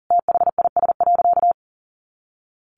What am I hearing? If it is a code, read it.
Morse code, heard as a keyed tone, about 40 wpm, 700 Hz. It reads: THIS.
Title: A Sound Of Thunder 40wpm